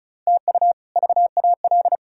Morse code, heard as a keyed tone, about 34 wpm, 700 Hz. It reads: TU VAL
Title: K1USN SST QSO Practice - 35wpm